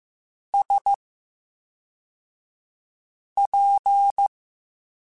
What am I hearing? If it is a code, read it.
Morse code, heard as a keyed tone, about 15 wpm, 800 Hz. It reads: S P
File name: Special INGs / Special INGs-mp3s / VerySlowToFaster Ing @10